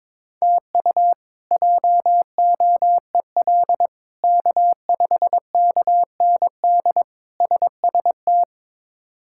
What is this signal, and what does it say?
Morse code, heard as a keyed tone, about 22 wpm, 700 Hz. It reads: TU JOEL K5KND SST
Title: K1USN SST QSO Practice - 22wpm